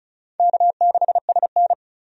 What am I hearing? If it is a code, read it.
Morse code, heard as a keyed tone, about 36 wpm, 700 Hz. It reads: K6SN
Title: USA Call Signs - 4 Characters - 35wpm